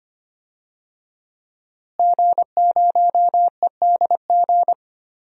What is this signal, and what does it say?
Morse code, heard as a keyed tone, about 25 wpm, 700 Hz. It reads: G0EDG
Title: DXCC-100 Call Signs - 5 Characters - Repeated 2x - 25wpm